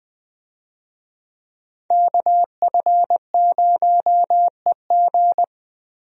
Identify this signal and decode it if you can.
Morse code, heard as a keyed tone, about 20 wpm, 700 Hz. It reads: KF0EG
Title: USA Call Signs - 5 Characters - Repeated 2x - 20wpm